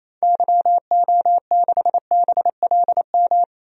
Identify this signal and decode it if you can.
Morse code, heard as a keyed tone, about 28 wpm, 700 Hz. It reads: YO6BLM
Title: DXCC-100 Call Signs - 6 Characters - 28wpm